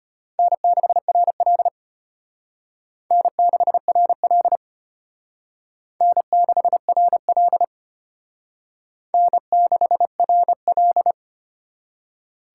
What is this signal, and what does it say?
Morse code, heard as a keyed tone, about 33 wpm, 700 Hz. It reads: N6RL N6RL N6RL N6RL